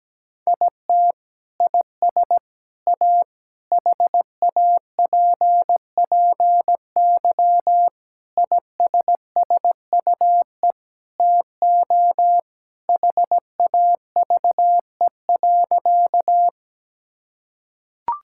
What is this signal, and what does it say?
Morse code, heard as a keyed tone, about 17 wpm, 700 Hz. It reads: IT IS A HAPPY ISSUE TO HAVE.
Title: Sentences from Top 1000 Words - Review Part 6 - 17wpm